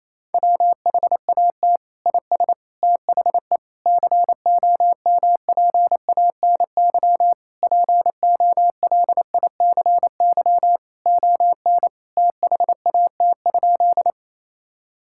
Morse code, heard as a keyed tone, about 28 wpm, 700 Hz. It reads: WHAT IS THE COMPANY POLICY ON THAT?
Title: Sentences from Top 1000 Words - Review Part 5 - 28wpm